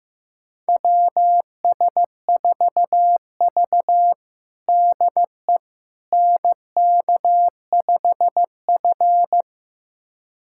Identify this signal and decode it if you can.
Morse code, heard as a keyed tone, about 15 wpm, 700 Hz. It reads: WS4V DE NK5F
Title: USA Callsign DE Callsign - 4 Characters - 15wpm